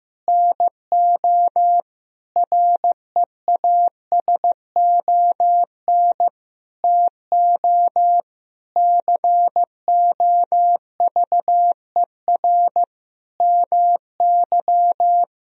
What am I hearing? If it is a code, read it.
Morse code, heard as a keyed tone, about 15 wpm, 700 Hz. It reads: NO REASON TO COVER MY
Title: Sentences from Top 1000 Words - Review Part 1 - 15wpm